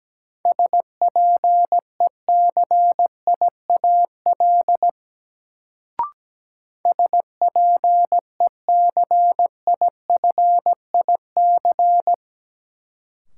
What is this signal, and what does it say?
Morse code, heard as a keyed tone, about 17 wpm, 700 Hz. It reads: SPECIAL SPECIFIC
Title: Sentences from Top 1000 Words - Review Part 3 - New Word Drill - 17wpm